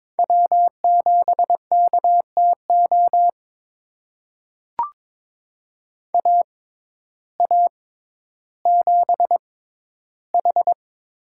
Morse code, heard as a keyed tone, about 22 wpm, spaced as for 12 wpm, 700 Hz. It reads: W7KTO AA7H
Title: USA Call Signs - 6x Character Spacing - 22wpm